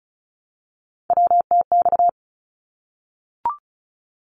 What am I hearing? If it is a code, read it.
Morse code, heard as a keyed tone, about 34 wpm, 700 Hz. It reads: WTX
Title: ARRL Sections 35wpm